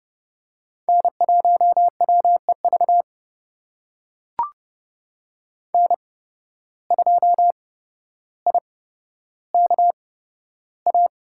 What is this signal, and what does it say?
Morse code, heard as a keyed tone, about 30 wpm, spaced as for 13 wpm, 700 Hz. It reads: N1WEV N2IKA
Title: USA Call Signs - 5 Characters - 8x Character Spacing - 30wpm